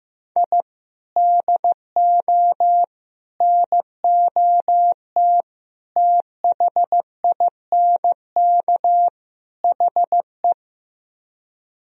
Morse code, heard as a keyed tone, about 15 wpm, 700 Hz. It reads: I DO NOT THINK HE
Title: Speed-Racing - Sets of 5 Words 15wpm